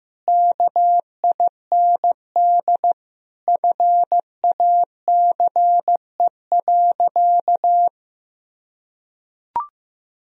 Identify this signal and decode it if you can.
Morse code, heard as a keyed tone, about 16 wpm, 700 Hz. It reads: KIND FACE.
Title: Sentences from Top 1000 Words - Review Part 2 - 15wpm